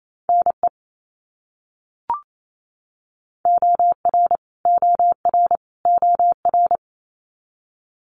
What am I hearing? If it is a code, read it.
Morse code, heard as a keyed tone, about 28 wpm, 700 Hz. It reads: NE OR OR OR
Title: US States Repeated 3x 28wpm